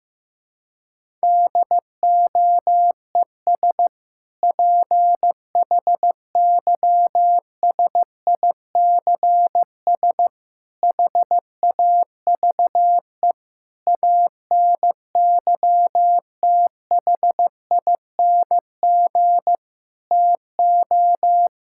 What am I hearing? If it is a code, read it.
Morse code, heard as a keyed tone, about 15 wpm, 700 Hz. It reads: DOES PHYSICS HAVE ANYTHING TO
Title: Sentences from Top 1000 Words - Review Part 5 - 15wpm